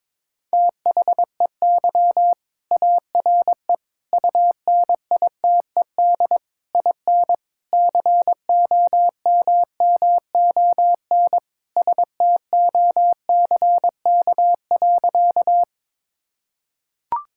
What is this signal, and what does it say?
Morse code, heard as a keyed tone, about 22 wpm, 700 Hz. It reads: THEY ARE UNITED IN COMMON STOCK.
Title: Sentences from Top 1000 Words - Review Part 5 - 22wpm